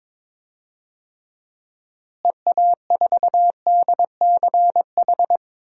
Morse code, heard as a keyed tone, about 21 wpm, 700 Hz. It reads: EA4DCH
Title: DXCC-100 Call Signs - 6 Characters - Repeated 2x - 22wpm